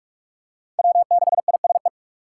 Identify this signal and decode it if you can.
Morse code, heard as a keyed tone, about 45 wpm, 700 Hz. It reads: W6ISE
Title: USA Call Signs - 5 Characters - 6x Character Spacing - 45wpm